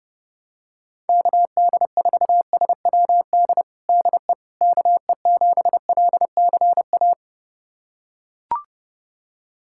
Morse code, heard as a keyed tone, about 32 wpm, 690 Hz. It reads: KD4SWD DE KE7LCA
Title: USA Callsign DE Callsign - 6 Characters - 30wpm